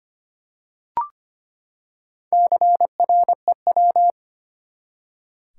Morse code, heard as a keyed tone, about 25 wpm, 700 Hz. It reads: CREW
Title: Four Letter Words - 25wpm